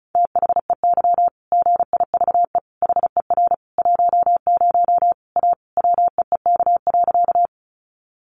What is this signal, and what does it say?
Morse code, heard as a keyed tone, about 34 wpm, 700 Hz. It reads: THEY GIVE HER 10 A WEEK.
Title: Easy Phrases 35wpm